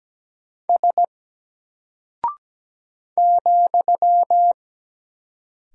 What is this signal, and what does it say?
Morse code, heard as a keyed tone, about 17 wpm, 700 Hz. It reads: S ,
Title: Single Character (Letter:Number:Punctuation:Prosign) Alt 17wpm